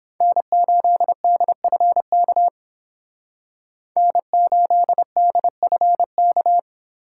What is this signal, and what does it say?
Morse code, heard as a keyed tone, about 28 wpm, 700 Hz. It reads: N8DFK N8DFK